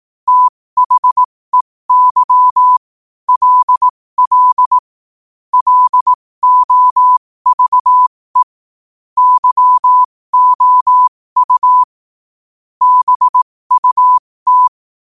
Morse code, heard as a keyed tone, about 18 wpm, spaced as for 15 wpm, 1000 Hz. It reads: THEY LL LOVE YOU BUT
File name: qotdfiles / 15wpm